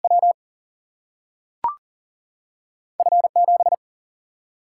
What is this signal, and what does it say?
Morse code, heard as a keyed tone, about 40 wpm, 700 Hz. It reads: W F 7
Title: Single-Letter Single-Number 40wpm